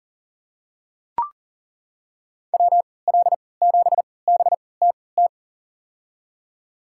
Morse code, heard as a keyed tone, about 40 wpm, spaced as for 23 wpm, 700 Hz. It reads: WL7BTT
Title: USA Call Signs - 6 Characters - 3x Character Spacing - 40wpm